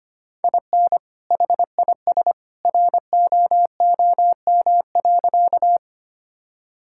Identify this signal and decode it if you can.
Morse code, heard as a keyed tone, about 25 wpm, 700 Hz. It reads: IN HIS ROOM.
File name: practice / 25wpm / Sentences with 2-Word Phrases - 25wpm